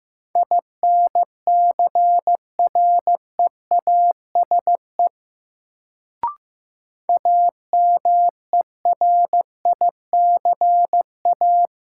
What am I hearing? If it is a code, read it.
Morse code, heard as a keyed tone, about 15 wpm, 700 Hz. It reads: INCREASE AMERICA
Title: Sets of 5 Words - New Word Drill - 15wpm